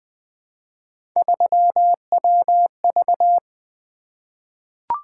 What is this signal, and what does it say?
Morse code, heard as a keyed tone, about 20 wpm, 700 Hz. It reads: 3WV
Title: Three Letter-Number Combinations - 20wpm